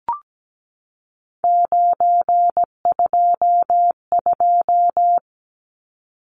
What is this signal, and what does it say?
Morse code, heard as a keyed tone, about 17 wpm, 700 Hz. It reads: 922
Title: Numbers - 3-Digit 17wpm